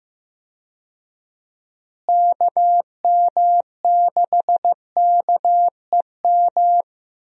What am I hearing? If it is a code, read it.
Morse code, heard as a keyed tone, about 15 wpm, 700 Hz. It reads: KM6KEM
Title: USA Call Signs - 6 Characters - Repeated 2x - 15wpm